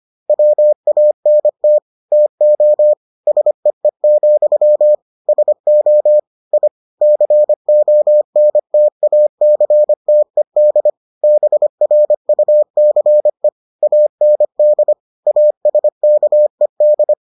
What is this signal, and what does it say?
Morse code, heard as a keyed tone, about 25 wpm, 600 Hz. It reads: WANT TO SEE, SO I CONTACTED BRUCE AND ASKED